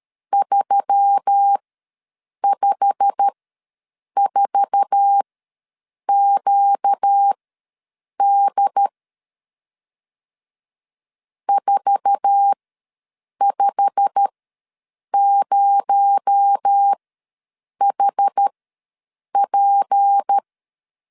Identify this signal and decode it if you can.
Morse code, heard as a keyed tone, about 13 wpm, spaced as for 7 wpm, 800 Hz. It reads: 354QD 450HP
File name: LEZIONI / 8 YJ45 / 8 YJ45 ws 12wpm 3146